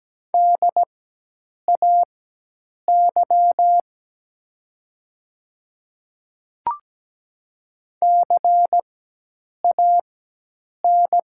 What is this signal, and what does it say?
Morse code, heard as a keyed tone, about 17 wpm, spaced as for 8 wpm, 700 Hz. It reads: DAY CAN
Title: 4x Character Spacing - Top 100 Words - 17wpm